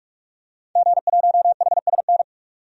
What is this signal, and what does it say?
Morse code, heard as a keyed tone, about 45 wpm, 700 Hz. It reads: G1HSN
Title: DXCC-100 Call Signs - 5 Characters - 45wpm